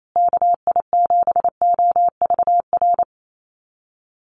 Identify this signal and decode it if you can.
Morse code, heard as a keyed tone, about 28 wpm, 700 Hz. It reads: KI7OVR
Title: USA Call Signs Repeated 3x 28wpm